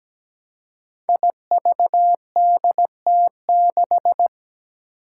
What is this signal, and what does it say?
Morse code, heard as a keyed tone, about 17 wpm, 700 Hz. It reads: IVDT6
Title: Five Challenging Letter-Number Combinations 17wpm